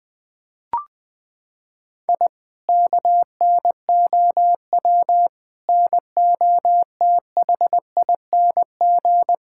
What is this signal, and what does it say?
Morse code, heard as a keyed tone, about 20 wpm, 700 Hz. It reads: I KNOW NOTHING
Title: Sentences from Top 1000 Words - Review Part 6 - 20wpm